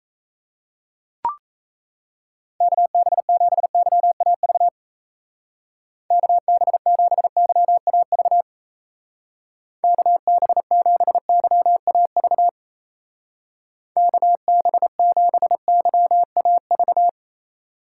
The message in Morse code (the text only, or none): KB7YAV KB7YAV KB7YAV KB7YAV